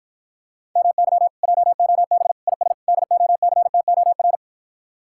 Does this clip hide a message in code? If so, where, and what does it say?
Yes: Morse code from 0.8 to 4.4 s, heard as a keyed tone, about 52 wpm, 700 Hz. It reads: MY JOB IS DOCTOR